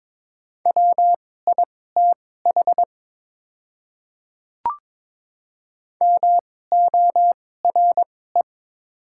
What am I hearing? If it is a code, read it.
Morse code, heard as a keyed tone, about 21 wpm, spaced as for 16 wpm, 690 Hz. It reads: WITH MORE
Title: 2x Character Spacing - Top 100 Words - 22wpm